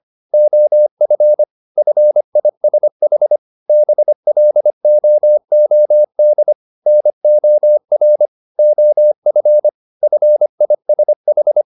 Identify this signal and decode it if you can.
Morse code, heard as a keyed tone, about 25 wpm, 600 Hz. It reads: OF FISH BLOOD NOR OF FISH